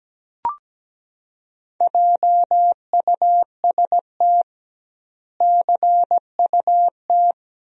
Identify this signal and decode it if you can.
Morse code, heard as a keyed tone, about 17 wpm, 700 Hz. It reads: JUST CUT
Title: 2x Word-Spacing - Sentences from Top 300 Words - 17wpm